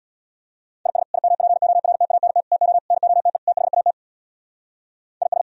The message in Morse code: IS IMPORTANT TO NOTE THAT IS

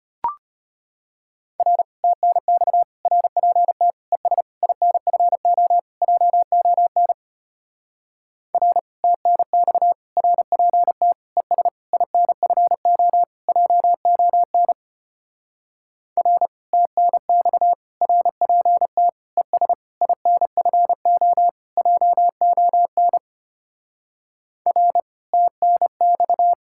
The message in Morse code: R TNX RPT ES INFO JON R TNX RPT ES INFO JON R TNX RPT ES INFO JON R TNX

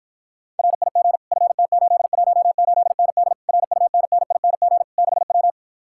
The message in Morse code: RIG FT818ND RUNNING 6W

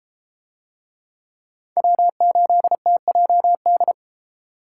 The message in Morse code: W8TJD

